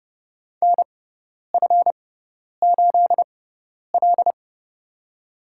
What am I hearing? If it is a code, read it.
Morse code, heard as a keyed tone, about 30 wpm, spaced as for 10 wpm, 700 Hz. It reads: NF8L
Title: USA Call Signs - 4 Characters - 6x Character Spacing - 30wpm